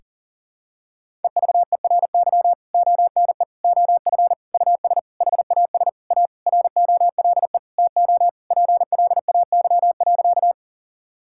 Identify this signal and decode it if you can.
Morse code, heard as a keyed tone, about 37 wpm, 700 Hz. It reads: EVERY ONE OF US HAS A ROLE TO PLAY.
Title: Sentences from Top 1000 Words - Review Part 6 - 40wpm